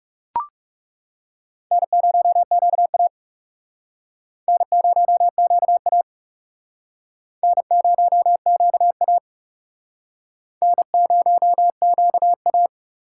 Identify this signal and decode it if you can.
Morse code, heard as a keyed tone, about 37 wpm, 700 Hz. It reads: N0QA N0QA N0QA N0QA
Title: Speed Racing - USA 1x2 Call Signs 30wpm